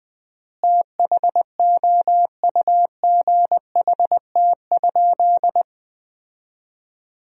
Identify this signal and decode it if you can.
Morse code, heard as a keyed tone, about 20 wpm, 700 Hz. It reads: THOUGHT?
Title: 4x Word-Spacing - Sentences from Top 300 Words - 20wpm